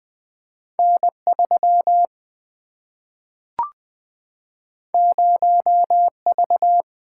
Morse code, heard as a keyed tone, about 20 wpm, 700 Hz. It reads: N 3 0 V
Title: Single-Letter Single-Number 20wpm